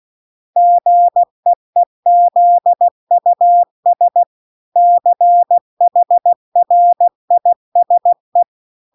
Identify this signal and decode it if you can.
Morse code, heard as a keyed tone, about 16 wpm, 700 Hz. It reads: GEEZUS CHRISE